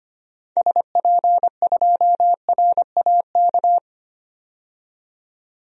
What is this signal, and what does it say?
Morse code, heard as a keyed tone, about 25 wpm, 700 Hz. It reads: SP2RAK
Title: DXCC-100 Call Signs - 6 Characters - 25wpm